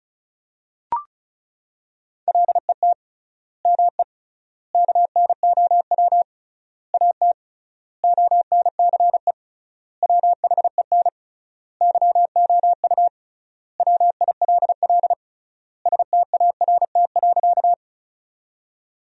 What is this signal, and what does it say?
Morse code, heard as a keyed tone, about 36 wpm, 700 Hz. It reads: LET ME KNOW AT ONCE WHEN YOU WILL START.
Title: 3x Word-Spacing - Sentences from Top 300 Words - 35wpm